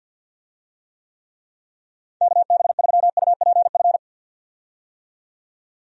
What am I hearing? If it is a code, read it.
Morse code, heard as a keyed tone, about 52 wpm, 700 Hz. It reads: KB3UPF
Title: USA Call Signs - 6 Characters - 50wpm